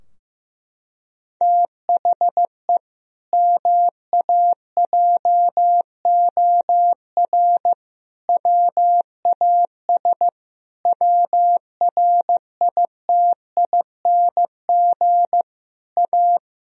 Morse code, heard as a keyed tone, about 16 wpm, 700 Hz. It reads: THE MAJOR WAS WRITING A
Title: Sentences from Top 1000 Words - Review Part 4 - 15wpm